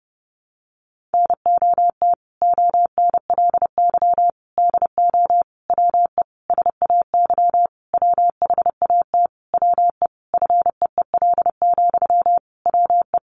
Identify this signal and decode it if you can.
Morse code, heard as a keyed tone, about 30 wpm, 700 Hz. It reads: NOT ONLY DO WE SAY WHAT WE FEEL, WE
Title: Sentences from Top 1200 Words 30wpm